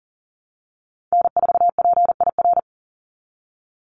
Morse code, heard as a keyed tone, about 40 wpm, 700 Hz. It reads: N4PIR
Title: Call Signs Repeated 2x - 40wpm